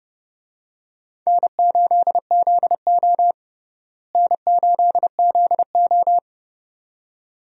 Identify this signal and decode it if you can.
Morse code, heard as a keyed tone, about 29 wpm, 700 Hz. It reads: N8ZO N8ZO
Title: USA Call Signs - 4 Characters - Repeated 2x - 30wpm